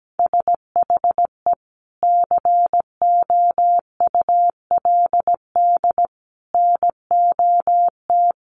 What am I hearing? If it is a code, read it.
Morse code, heard as a keyed tone, about 17 wpm, 700 Hz. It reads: SHE COULD NOT
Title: Sentences from Top 1400 Words 17wpm